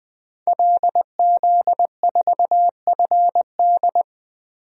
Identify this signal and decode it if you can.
Morse code, heard as a keyed tone, about 20 wpm, 700 Hz. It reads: LZ4FD
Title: DXCC-100 Call Signs - 5 Characters - Repeated 3x - 20wpm